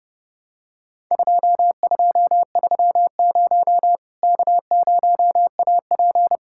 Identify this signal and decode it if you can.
Morse code, heard as a keyed tone, about 30 wpm, 700 Hz. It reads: 2230 K0AP
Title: Worked All Europe DX Contest - QTC Exchange - 30wpm